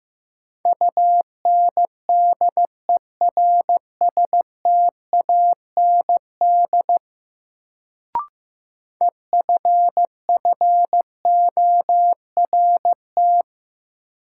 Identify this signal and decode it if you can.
Morse code, heard as a keyed tone, about 15 wpm, 700 Hz. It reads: UNDERSTAND EFFORT
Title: Sets of 4 Words - New Word Drill - 15wpm